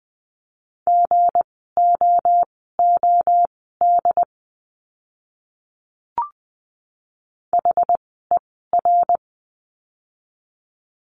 Morse code, heard as a keyed tone, about 20 wpm, spaced as for 15 wpm, 700 Hz. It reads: GOOD HER
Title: 230 Lesson 37f - Letter x - Words 20wpm